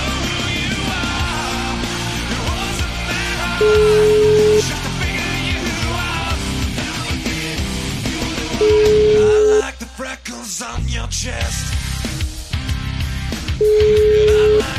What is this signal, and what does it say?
Ringback tone (Europe, 425 Hz), with noise.